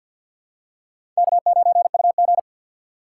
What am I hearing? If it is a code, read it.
Morse code, heard as a keyed tone, about 53 wpm, 700 Hz. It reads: K9UG
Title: USA Call Signs - 2x Character Spacing - 50wpm